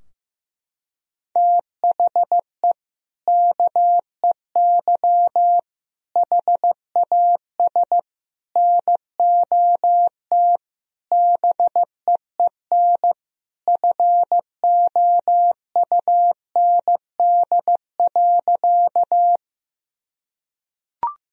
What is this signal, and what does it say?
Morse code, heard as a keyed tone, about 15 wpm, 700 Hz. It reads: THE KEY HAS NOT BEEN FOUND.
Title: Sentences from Top 1000 Words - Review Part 4 - 15wpm